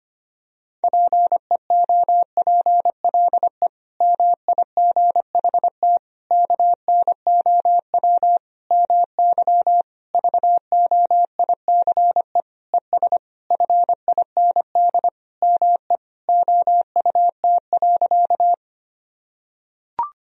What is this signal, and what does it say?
Morse code, heard as a keyed tone, about 25 wpm, 700 Hz. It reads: PEOPLE MIGHT KNOW MY VOICE ES FIND ME OUT.
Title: Sentences from Top 500 Words and Abbreviations 25wpm